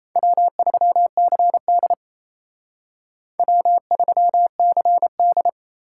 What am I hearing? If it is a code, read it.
Morse code, heard as a keyed tone, about 35 wpm, 700 Hz. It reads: W3CD W3CD